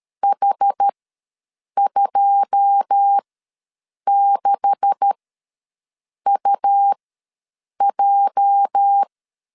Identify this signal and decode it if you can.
Morse code, heard as a keyed tone, about 13 wpm, spaced as for 7 wpm, 800 Hz. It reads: H26UJ
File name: LEZIONI / 10 SIGNS / 10C DOT ws 12wpm 3548